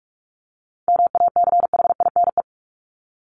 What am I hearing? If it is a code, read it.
Morse code, heard as a keyed tone, about 44 wpm, 700 Hz. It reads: MACHINE
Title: Top 500 Words Repeated 45wpm